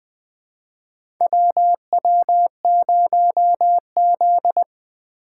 Morse code, heard as a keyed tone, about 21 wpm, 700 Hz. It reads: WW0Z
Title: USA Call Signs - 4 Characters - 2x Character Spacing - 20wpm